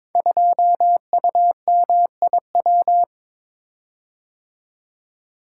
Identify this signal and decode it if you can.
Morse code, heard as a keyed tone, about 22 wpm, 700 Hz. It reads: 2UMIW
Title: Code Groups - 5 Letter:Numbers - 22wpm